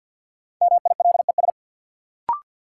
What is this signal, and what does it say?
Morse code, heard as a keyed tone, about 46 wpm, 700 Hz. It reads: MILES
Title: Sentences from Top 1000 Words - Review Part 3 - New Word Drill - 50wpm